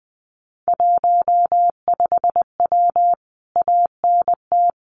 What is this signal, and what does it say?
Morse code, heard as a keyed tone, about 19 wpm, 700 Hz. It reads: 15W ANT
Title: Common QSO Phrases Encore 20wpm